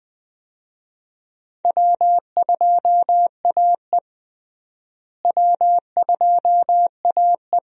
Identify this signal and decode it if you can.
Morse code, heard as a keyed tone, about 20 wpm, 700 Hz. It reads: W2AE W2AE
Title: USA Call Signs - 4 Characters - Repeated 2x - 20wpm